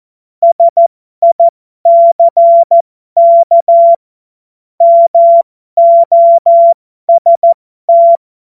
Morse code, heard as a keyed tone, about 14 wpm, spaced as for 12 wpm, 680 Hz. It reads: SICK MOST
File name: Morsekurs / Witze